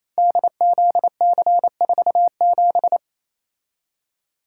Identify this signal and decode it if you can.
Morse code, heard as a keyed tone, about 28 wpm, 700 Hz. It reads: DZC47
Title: Code Groups - 5 Letter:Numbers - 28wpm